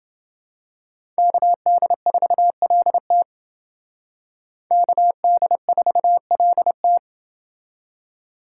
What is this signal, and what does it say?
Morse code, heard as a keyed tone, about 28 wpm, 700 Hz. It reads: KD4LT KD4LT